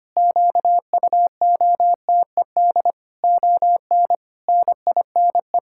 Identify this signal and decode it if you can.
Morse code, heard as a keyed tone, about 25 wpm, 700 Hz. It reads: QUOTED ON NINE
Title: Pangram Challenge v2 - 25wpm